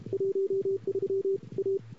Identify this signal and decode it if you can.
Morse code, heard as a keyed tone, about 36 wpm, 410 Hz. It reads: 13 A
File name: CW / 2008-ss-cw-k5tr-op-n5rz -21:00